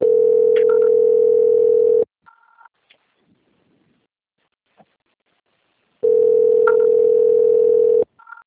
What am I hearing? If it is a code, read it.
Ringback tone (North America), 440 + 480 Hz.
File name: out- - - - - .30561